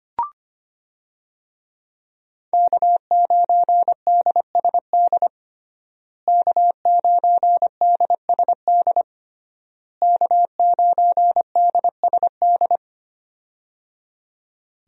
Morse code, heard as a keyed tone, about 25 wpm, 700 Hz. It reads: K9DSD K9DSD K9DSD
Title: USA Call Signs - 5 Characters - Repeated 3x - 25wpm